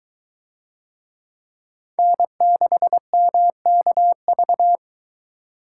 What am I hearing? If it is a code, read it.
Morse code, heard as a keyed tone, about 22 wpm, 700 Hz. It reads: N6MKV